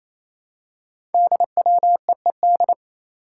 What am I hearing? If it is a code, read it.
Morse code, heard as a keyed tone, about 27 wpm, 700 Hz. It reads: DWEED
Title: Code Groups - 5 Letter:Numbers - 28wpm